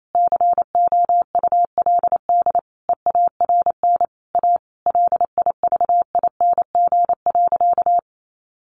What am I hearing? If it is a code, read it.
Morse code, heard as a keyed tone, about 28 wpm, 700 Hz. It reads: COULD EARN A LIVING.
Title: Sentences from Top 1500 Words 28wpm